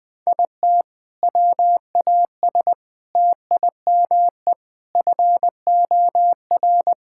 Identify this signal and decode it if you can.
Morse code, heard as a keyed tone, about 21 wpm, 700 Hz. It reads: IT WAS TIME FOR
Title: Sentences from Top 1600 Words 20wpm